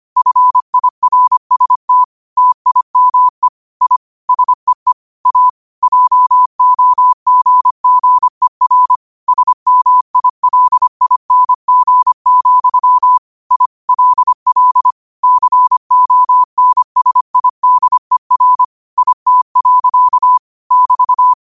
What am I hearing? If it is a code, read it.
Morse code, heard as a keyed tone, about 25 wpm, 1000 Hz. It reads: FIRST TIME I SEE A JOGGER SMILING, I LL CONSIDER IT. <BT>
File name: qotdfiles / 25wpm